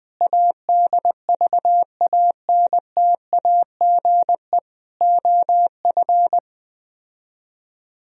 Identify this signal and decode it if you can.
Morse code, heard as a keyed tone, about 20 wpm, 700 Hz. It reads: ADVANTAGE OF